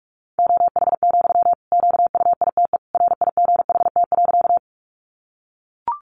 Morse code, heard as a keyed tone, about 44 wpm, 700 Hz. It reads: OH, QUITE RIGHT.
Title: Sentences from Top 700 Words 45wpm